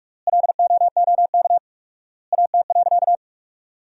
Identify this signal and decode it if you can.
Morse code, heard as a keyed tone, about 46 wpm, 700 Hz. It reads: LOOK AT.
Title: 4x Word-Spacing - Sentences from Top 200 Words - 45wpm